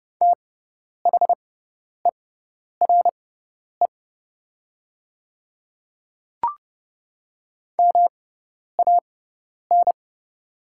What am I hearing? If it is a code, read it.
Morse code, heard as a keyed tone, about 30 wpm, spaced as for 10 wpm, 700 Hz. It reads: THERE MAN
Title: 6x Character Spacing - Top 200 Words - 30wpm